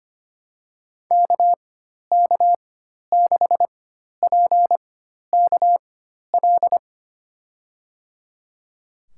Morse code, heard as a keyed tone, about 25 wpm, spaced as for 12 wpm, 700 Hz. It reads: KK6PKL
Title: USA Call Signs - 6 Characters - 4x Character Spacing - 25wpm